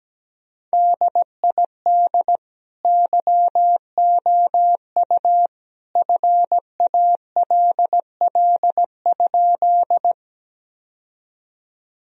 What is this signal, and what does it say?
Morse code, heard as a keyed tone, about 17 wpm, 700 Hz. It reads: DID YOU FALL?
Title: Sentences from Top 1000 Words - Review Part 2 - 17wpm